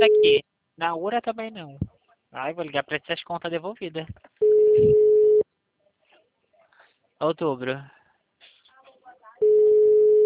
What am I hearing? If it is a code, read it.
Ringback tone (Europe, 425 Hz), with speech.